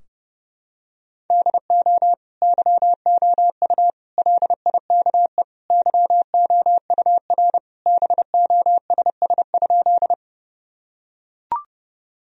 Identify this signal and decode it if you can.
Morse code, heard as a keyed tone, about 30 wpm, 700 Hz. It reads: DO YOU LIKE YOUR BOSS?
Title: Sentences from Top 1000 Words - Review Part 4 - 30wpm